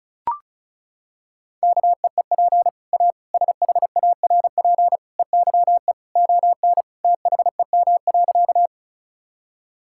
Morse code, heard as a keyed tone, about 36 wpm, 700 Hz. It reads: KEEP A SHARP EYE ON THEM.
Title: Sentences from Top 1600 Words 35wpm